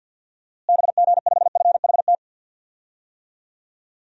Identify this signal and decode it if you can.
Morse code, heard as a keyed tone, about 51 wpm, 700 Hz. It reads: BG5FHT
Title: DXCC-100 Call Signs - 6 Characters - Repeated 3x - 50wpm